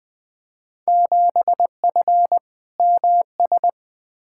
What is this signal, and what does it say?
Morse code, heard as a keyed tone, about 20 wpm, 700 Hz. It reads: 7F MS
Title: Field Day Exchanges Encore - 20wpm